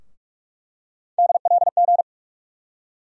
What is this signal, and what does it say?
Morse code, heard as a keyed tone, about 46 wpm, 700 Hz. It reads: DLG
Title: USA Call Sign Postfixes - 3 Letters 45wpm